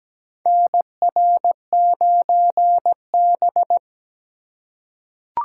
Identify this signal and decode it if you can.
Morse code, heard as a keyed tone, about 17 wpm, 700 Hz. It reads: NR9B
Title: USA Call Signs - 4 Characters - 3x Character Spacing - 17wpm